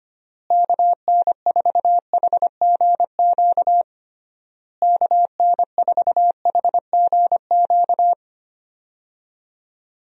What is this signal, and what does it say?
Morse code, heard as a keyed tone, about 25 wpm, 700 Hz. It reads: KN4HGQ KN4HGQ
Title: USA Call Signs - 6 Characters - Repeated 3x - 25wpm